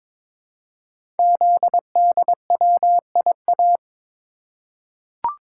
Morse code, heard as a keyed tone, about 22 wpm, 700 Hz. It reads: ZDWIA
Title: Code Groups - 5 Letters - 22wpm